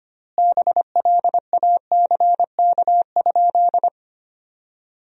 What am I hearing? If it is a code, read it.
Morse code, heard as a keyed tone, about 25 wpm, 700 Hz. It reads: BLACK?
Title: 4x Word-Spacing - Sentences from Top 400 Words - 25wpm